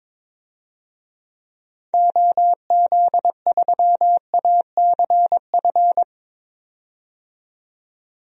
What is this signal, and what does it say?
Morse code, heard as a keyed tone, about 22 wpm, 700 Hz. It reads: OZ3ACF
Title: DXCC-100 Call Signs - 6 Characters - 22wpm